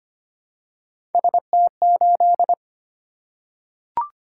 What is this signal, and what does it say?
Morse code, heard as a keyed tone, about 25 wpm, 700 Hz. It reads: ST8
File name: Three Letter-Number Combinations - 25wpm